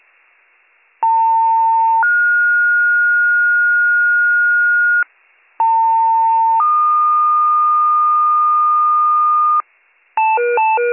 Two-tone page, heard as a one-second tone, then a three-second tone, about 900 then 1430 Hz, about 900 then 1220 Hz, with noise. It is heard from 1.0 to 9.6 s.